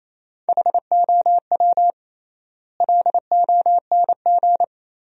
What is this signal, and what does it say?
Morse code, heard as a keyed tone, about 28 wpm, 700 Hz. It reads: HOW LONG
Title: 3x Word-Spacing - Sentences from Top 500 Words - 28wpm